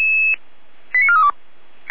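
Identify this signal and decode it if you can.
Five-tone selective call (ZVEI), heard as quick runs of five tones, tones of about 70 ms. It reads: 89432
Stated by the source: Received by SDR.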